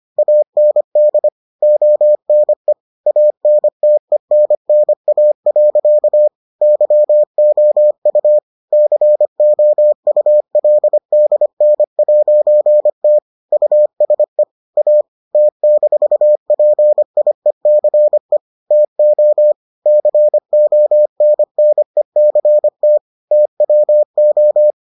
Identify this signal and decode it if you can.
Morse code, heard as a keyed tone, about 25 wpm, 600 Hz. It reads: AND ONE ANTENNA. YOU COULDN'T USE A T-PIECE TO CONNECT TWO